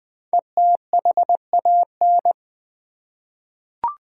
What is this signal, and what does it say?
Morse code, heard as a keyed tone, about 20 wpm, 700 Hz. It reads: ETHAN
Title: USA Common Names - 20wpm